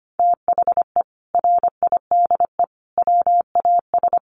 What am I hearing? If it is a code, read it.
Morse code, heard as a keyed tone, about 25 wpm, 700 Hz. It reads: THE RIDE WAS
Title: Sentences from Top 1200 Words 25wpm